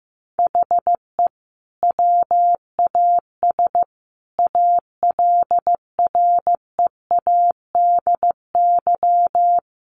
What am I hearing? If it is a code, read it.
Morse code, heard as a keyed tone, about 15 wpm, 700 Hz. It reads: HE WAS ALREADY
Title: Sentences from Top 1100 Words 15wpm